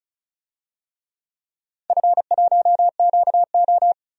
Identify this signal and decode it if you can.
Morse code, heard as a keyed tone, about 34 wpm, 700 Hz. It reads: F1QO
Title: DXCC-100 Call Signs - 4 Characters - 35wpm